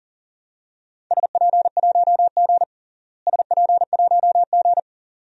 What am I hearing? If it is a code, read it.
Morse code, heard as a keyed tone, about 40 wpm, 700 Hz. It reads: SP1G SP1G
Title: DXCC-100 Call Signs - 4 Characters - Repeated 2x - 40wpm